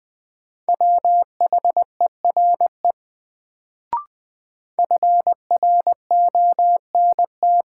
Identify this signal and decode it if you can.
Morse code, heard as a keyed tone, about 20 wpm, 700 Hz. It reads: WHERE FRONT
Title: Sets of 2 Words - New Word Drill - 20wpm